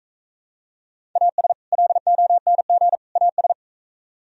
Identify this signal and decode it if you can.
Morse code, heard as a keyed tone, about 44 wpm, 700 Hz. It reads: AS LONG AS